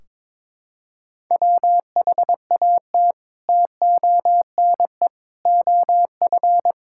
Morse code, heard as a keyed tone, about 22 wpm, 700 Hz. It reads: WHAT TONE OF
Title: Sentences from Top 1600 Words 22wpm